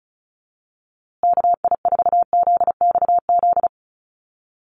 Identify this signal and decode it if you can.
Morse code, heard as a keyed tone, about 34 wpm, 700 Hz. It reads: KI4ZXZ
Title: USA Call Signs Repeated 2x 35wpm